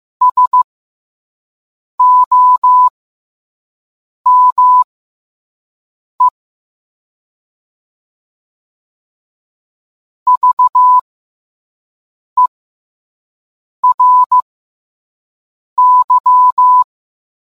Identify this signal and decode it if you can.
Morse code, heard as a keyed tone, about 15 wpm, spaced as for 5 wpm, 1000 Hz. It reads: SOME VERY